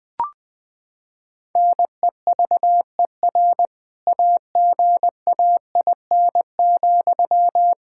Morse code, heard as a keyed tone, about 20 wpm, 700 Hz. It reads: NEVER AGAIN,
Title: Sentences from Top 1600 Words 20wpm